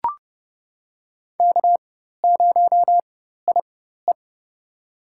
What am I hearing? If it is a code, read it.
Morse code, heard as a keyed tone, about 30 wpm, spaced as for 14 wpm, 700 Hz. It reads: K0IE
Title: USA Call Signs - 4 Characters - 4x Character Spacing - 30wpm